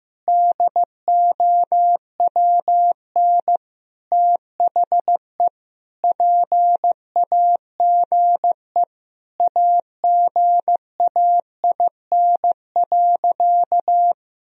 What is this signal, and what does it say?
Morse code, heard as a keyed tone, about 15 wpm, 700 Hz. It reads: DOWN THE PAGE AGAIN.
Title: Sentences from Top 1000 Words - Review Part 1 - 15wpm